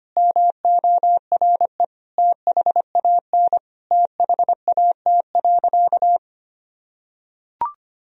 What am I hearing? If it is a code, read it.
Morse code, heard as a keyed tone, about 25 wpm, 700 Hz. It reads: MORE THAN THAT.
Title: Sentences from Top 1000 Words - Review Part 3 - 25wpm